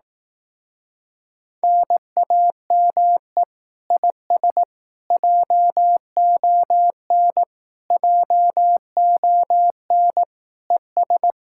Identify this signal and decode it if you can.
Morse code, heard as a keyed tone, about 18 wpm, 700 Hz. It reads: NAME IS JON JON ES
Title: Speed-Racing Common QSO Phrases Encore - 15wpm